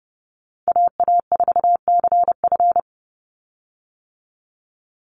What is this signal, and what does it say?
Morse code, heard as a keyed tone, about 30 wpm, 700 Hz. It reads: AA4CF
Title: Call Signs - 30wpm